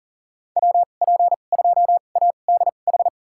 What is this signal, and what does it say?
Morse code, heard as a keyed tone, about 37 wpm, spaced as for 28 wpm, 700 Hz. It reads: WP2ADH
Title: USA Call Signs - 6 Characters - 2x Character Spacing - 40wpm